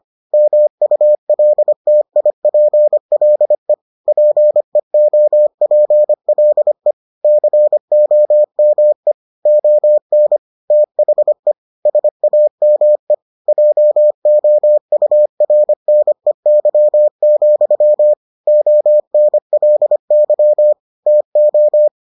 Morse code, heard as a keyed tone, about 25 wpm, 600 Hz. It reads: MULTIPLE PEOPLE COME ON THE SAME JOURNEY, ONLY TO